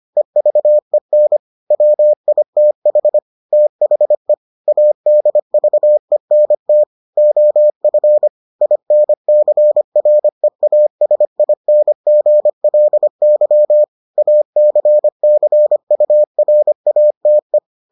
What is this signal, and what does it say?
Morse code, heard as a keyed tone, about 25 wpm, 600 Hz. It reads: EVEN WITH THE ADVENT OF INCREASINGLY ACCURATE